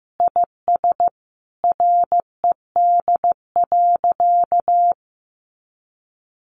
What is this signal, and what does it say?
Morse code, heard as a keyed tone, about 15 wpm, 700 Hz. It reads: IS RED.
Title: Additional Easy Phrases 15wpm